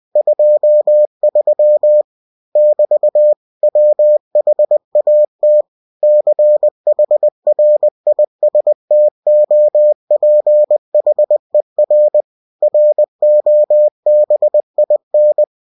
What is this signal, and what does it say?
Morse code, heard as a keyed tone, about 20 wpm, 600 Hz. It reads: 23 <BT> WHAT CHRISTOPHER ROBIN